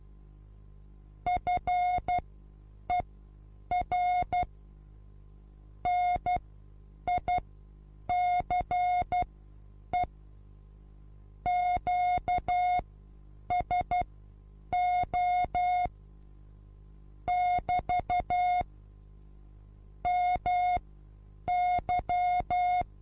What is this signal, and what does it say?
Morse code, heard as a keyed tone, about 12 wpm, spaced as for 8 wpm, 720 Hz. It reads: FER NICE QSO <BT> MY
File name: Onlinelehrgang / morsen / DJ4UF Morsen23